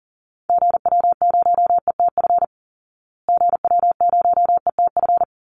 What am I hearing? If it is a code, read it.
Morse code, heard as a keyed tone, about 37 wpm, 700 Hz. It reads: GW0ETF GW0ETF
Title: Call Signs Repeated 3x - 40wpm